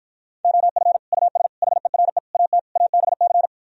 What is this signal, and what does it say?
Morse code, heard as a keyed tone, about 52 wpm, 700 Hz. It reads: OF US HERE AT ABC